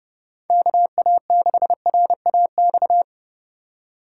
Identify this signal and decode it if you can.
Morse code, heard as a keyed tone, about 29 wpm, 700 Hz. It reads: KA6RAX
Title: Speed Racing - USA 2x3 Call Signs 30wpm